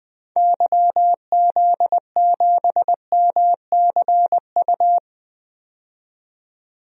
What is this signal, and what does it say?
Morse code, heard as a keyed tone, about 20 wpm, 700 Hz. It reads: YZ7MCU
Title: DXCC-100 Call Signs - 6 Characters - 20wpm